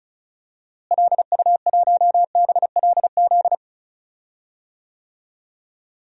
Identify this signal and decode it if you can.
Morse code, heard as a keyed tone, about 36 wpm, 700 Hz. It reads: LU1BLZ
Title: DXCC-100 Call Signs - 6 Characters - Repeated 3x - 35wpm